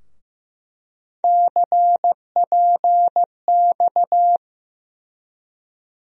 Morse code, heard as a keyed tone, about 16 wpm, 700 Hz. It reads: CPX
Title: USA Call Sign Postfixes - 3 Letters 15wpm